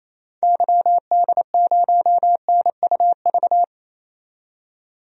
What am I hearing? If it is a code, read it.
Morse code, heard as a keyed tone, about 28 wpm, 700 Hz. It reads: YD0NUV
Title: DXCC-100 Call Signs - 6 Characters - 28wpm